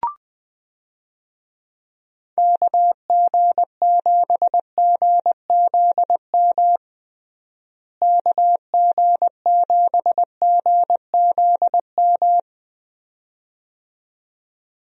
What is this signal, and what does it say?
Morse code, heard as a keyed tone, about 20 wpm, 700 Hz. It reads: KG7GZM KG7GZM
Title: USA Call Signs - 6 Characters - Repeated 2x - 20wpm